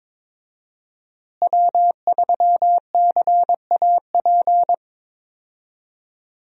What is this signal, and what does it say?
Morse code, heard as a keyed tone, about 22 wpm, 700 Hz. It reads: W3CAP
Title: USA Call Signs - 5 Characters - Repeated 3x - 22wpm